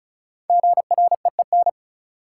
Morse code, heard as a keyed tone, about 34 wpm, 700 Hz. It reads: GREEN
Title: Sentences from Top 500 Words - New Word Drill - 35wpm